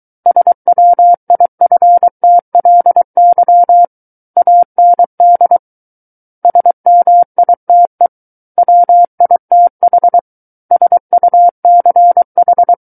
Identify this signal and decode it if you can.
Morse code, heard as a keyed tone, about 23 wpm, 700 Hz. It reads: SWIFTLY AND SMITE WITH SUCH
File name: Book One / 14-Chapter 14 - 23 WPM